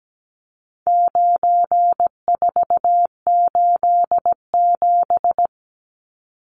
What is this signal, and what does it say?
Morse code, heard as a keyed tone, about 17 wpm, 700 Hz. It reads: 9487
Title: Numbers - 4-Digit 17wpm